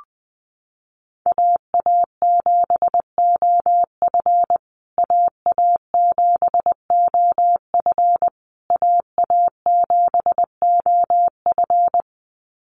Morse code, heard as a keyed tone, about 19 wpm, 700 Hz. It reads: AA7OF AA7OF AA7OF
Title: USA Call Signs Repeated 3x 20wpm